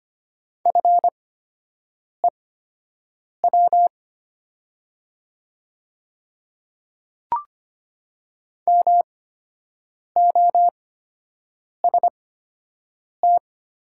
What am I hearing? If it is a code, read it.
Morse code, heard as a keyed tone, about 25 wpm, spaced as for 7 wpm, 700 Hz. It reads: FEW MOST
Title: 8x Character Spacing - Top 300 Words - 25wpm